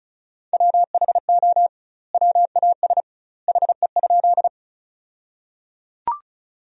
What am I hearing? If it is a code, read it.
Morse code, heard as a keyed tone, about 36 wpm, 700 Hz. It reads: WHO WAS HE?
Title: 2x Word-Spacing - Sentences from Top 100 Words - 35wpm